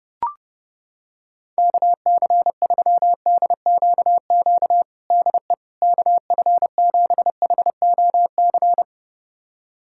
Morse code, heard as a keyed tone, about 31 wpm, 700 Hz. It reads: KC3DQQ DE KF7HOC
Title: USA Callsign DE Callsign 30wpm